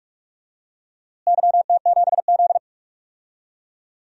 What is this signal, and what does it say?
Morse code, heard as a keyed tone, about 47 wpm, 700 Hz. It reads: YT7Z
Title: DXCC-100 Call Signs - 4 Characters - Repeated 2x - 45wpm